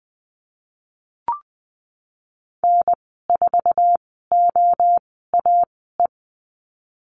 Morse code, heard as a keyed tone, about 20 wpm, spaced as for 15 wpm, 700 Hz. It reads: N4OAE
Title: 055 Lesson 10g - Letter r - Call Signs 20wpm